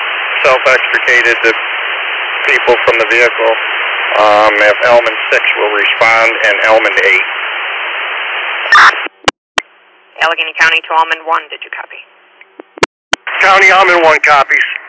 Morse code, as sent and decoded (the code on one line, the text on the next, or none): none